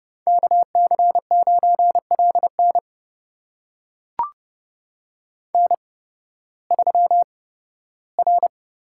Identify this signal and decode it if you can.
Morse code, heard as a keyed tone, about 32 wpm, spaced as for 16 wpm, 700 Hz. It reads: KC9LN N3R
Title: USA Call Signs - 5 Characters - 8x Character Spacing - 30wpm